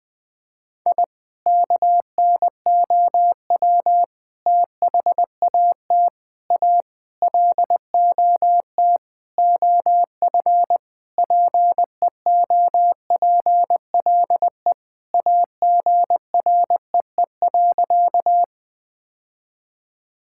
Morse code, heard as a keyed tone, about 20 wpm, 700 Hz. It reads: I KNOW THAT A LOT OF PEOPLE AGREE.
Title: Sentences with 5-Word Phrases - 20wpm